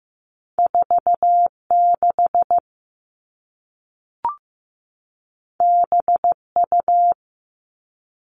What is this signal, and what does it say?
Morse code, heard as a keyed tone, about 15 wpm, 700 Hz. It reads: 46 BU
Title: All Cut Numbers - 2-Digit 15wpm